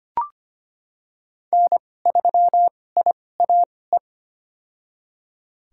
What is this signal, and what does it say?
Morse code, heard as a keyed tone, about 25 wpm, spaced as for 18 wpm, 700 Hz. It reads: N3IAE
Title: USA Call Signs - 2x Character Spacing - 25wpm